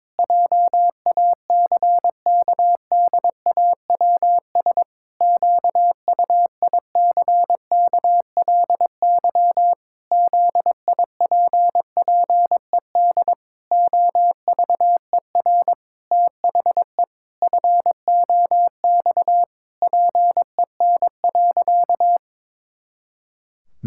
Morse code, heard as a keyed tone, about 22 wpm, 700 Hz. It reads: JACKDAWS QUICKLY ZIPPED OVER THE FOX PEN.
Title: Pangram Challenge v2 - 22wpm